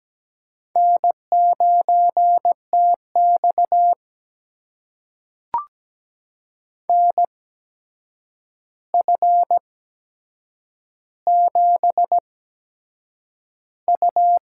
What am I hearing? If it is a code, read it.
Morse code, heard as a keyed tone, about 17 wpm, spaced as for 7 wpm, 700 Hz. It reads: N9TX NF7U
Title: USA Call Signs - 4 Characters - 8x Character Spacing - 17wpm